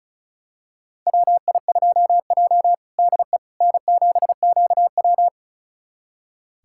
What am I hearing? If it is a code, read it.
Morse code, heard as a keyed tone, about 36 wpm, 700 Hz. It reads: WI2J DE N7QW
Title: USA Callsign DE Callsign - 4 Characters - 35wpm